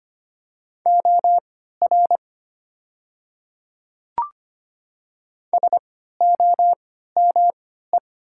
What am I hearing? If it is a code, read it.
Morse code, heard as a keyed tone, about 25 wpm, spaced as for 14 wpm, 700 Hz. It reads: OR SOME
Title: 3x Character Spacing - Top 100 Words - 25wpm